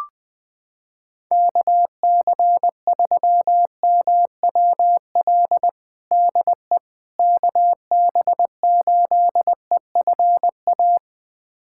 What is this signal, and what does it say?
Morse code, heard as a keyed tone, about 20 wpm, 700 Hz. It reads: KC3MWL DE KB8EFA
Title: USA Callsign DE Callsign - 6 Characters - 20wpm